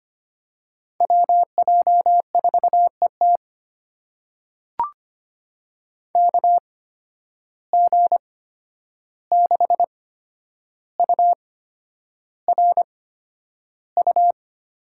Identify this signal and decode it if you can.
Morse code, heard as a keyed tone, about 25 wpm, spaced as for 10 wpm, 700 Hz. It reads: WJ4ET KG6URU
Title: USA Call Signs - 8x Character Spacing - 25wpm